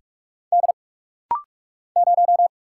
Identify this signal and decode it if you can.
Morse code, heard as a keyed tone, about 45 wpm, 700 Hz. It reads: D 0
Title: Single Letter-Number Alt - Rapid-Fire 45wpm